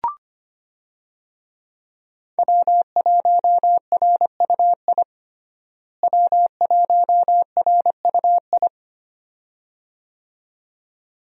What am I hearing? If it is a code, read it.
Morse code, heard as a keyed tone, about 25 wpm, 700 Hz. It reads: W1RUI W1RUI
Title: USA Call Signs - 5 Characters - Repeated 2x - 25wpm